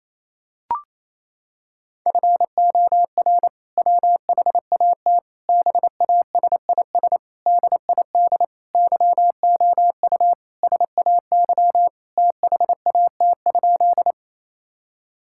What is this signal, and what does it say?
Morse code, heard as a keyed tone, about 28 wpm, 700 Hz. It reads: FOR WHAT BASIS DID YOU SAY THAT?
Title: Sentences from Top 1000 Words - Review Part 5 - 28wpm